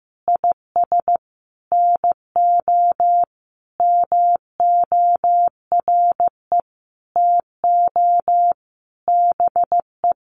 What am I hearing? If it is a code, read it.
Morse code, heard as a keyed tone, about 14 wpm, 700 Hz. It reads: IS NO MORE TO BE
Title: Sentences from Top 100 Words 15wpm